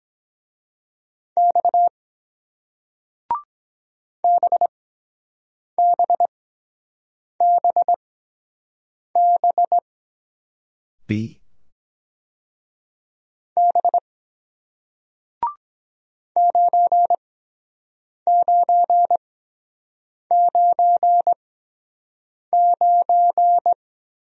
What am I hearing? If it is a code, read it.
Morse code, heard as a keyed tone, about 23 wpm, spaced as for 7 wpm, 700 Hz. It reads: X BBBB B 9999